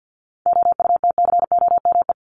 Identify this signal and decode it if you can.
Morse code, heard as a keyed tone, about 48 wpm, 700 Hz. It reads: OUTCOME
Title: Top 1501-1750 Words 50wpm